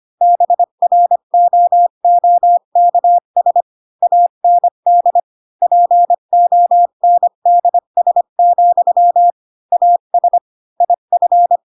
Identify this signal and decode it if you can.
Morse code, heard as a keyed tone, about 25 wpm, 700 Hz. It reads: BROOKS AND PONDS, AS IF
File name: Pinocchio / Pinocchio ch4 20 25 ik1hge